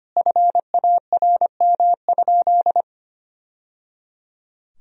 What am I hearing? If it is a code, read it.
Morse code, heard as a keyed tone, about 25 wpm, 700 Hz. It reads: FARM?
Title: Sentences from Top 1000 Words - Review Part 2 - 25wpm